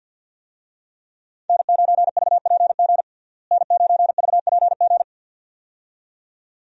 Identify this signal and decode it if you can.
Morse code, heard as a keyed tone, about 52 wpm, 700 Hz. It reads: N9VPG N9VPG
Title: USA Call Signs - 5 Characters - Repeated 2x - 50wpm